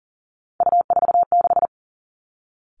Morse code, heard as a keyed tone, about 38 wpm, 700 Hz. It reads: U46
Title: Three Challenging Letter-Number Combinations 40wpm